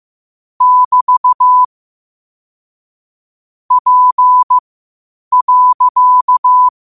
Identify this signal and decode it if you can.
Morse code, heard as a keyed tone, about 15 wpm, spaced as for 8 wpm, 1000 Hz. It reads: <BT> P.
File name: qotdfiles / 8wpm